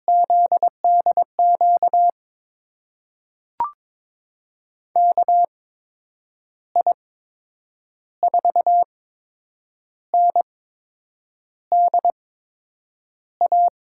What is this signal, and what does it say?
Morse code, heard as a keyed tone, about 21 wpm, spaced as for 8 wpm, 700 Hz. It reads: ZDQ KI4NDA
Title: USA Call Signs - 6 Characters - 8x Character Spacing - 22wpm